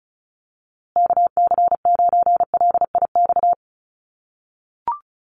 Morse code, heard as a keyed tone, about 34 wpm, 700 Hz. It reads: KC9LIX
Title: USA Call Signs Repeated 3x 35wpm